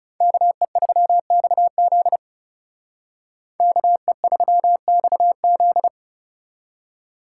Morse code, heard as a keyed tone, about 38 wpm, 700 Hz. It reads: KE3XZ KE3XZ